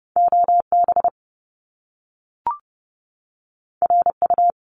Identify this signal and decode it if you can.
Morse code, heard as a keyed tone, about 30 wpm, 700 Hz. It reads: OB RU
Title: Top Two Letter Combinations in Words 30wpm